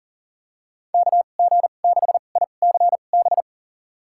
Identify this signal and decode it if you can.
Morse code, heard as a keyed tone, about 37 wpm, spaced as for 28 wpm, 700 Hz. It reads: KG6ICB
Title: USA Call Signs - 2x Character Spacing - 40wpm